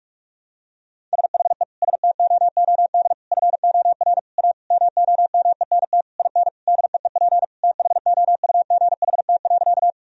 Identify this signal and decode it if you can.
Morse code, heard as a keyed tone, about 41 wpm, 700 Hz. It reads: SHE STOOD FOR A MOMENT IN DEEP THOUGHT.
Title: Sentences from Top 1000 Words - Review Part 3 - 45wpm